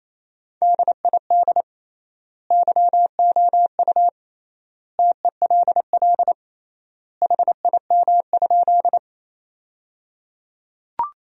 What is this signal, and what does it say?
Morse code, heard as a keyed tone, about 28 wpm, 700 Hz. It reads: DID YOU TELL HIM?
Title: 3x Word-Spacing - Sentences from Top 200 Words - 28wpm